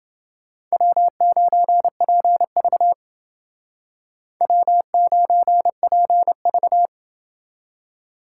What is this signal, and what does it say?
Morse code, heard as a keyed tone, about 29 wpm, 700 Hz. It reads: W9PV W9PV